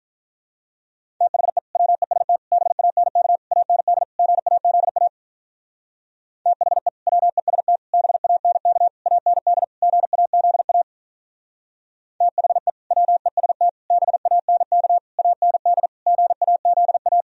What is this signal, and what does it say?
Morse code, heard as a keyed tone, about 44 wpm, 700 Hz. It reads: THE WEST BANK AND GAZA THE WEST BANK AND GAZA THE WEST BANK AND GAZA